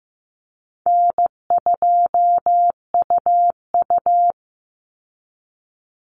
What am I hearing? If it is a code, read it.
Morse code, heard as a keyed tone, about 15 wpm, 700 Hz. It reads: N2UU
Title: Call Signs Repeated 2x - 15wpm